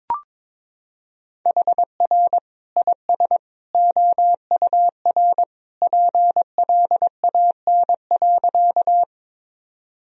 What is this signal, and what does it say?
Morse code, heard as a keyed tone, about 22 wpm, 700 Hz. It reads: HR IS OUR PLAN.
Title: Sentences from Top 500 Words and Abbreviations 22wpm